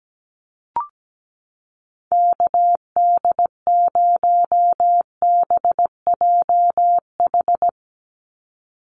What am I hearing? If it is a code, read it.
Morse code, heard as a keyed tone, about 17 wpm, 700 Hz. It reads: KD0BJH
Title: USA Call Signs 17wpm